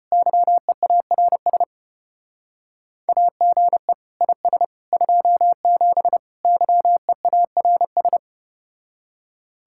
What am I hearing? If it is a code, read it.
Morse code, heard as a keyed tone, about 34 wpm, 700 Hz. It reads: YEARS AGE IS 27 YEARS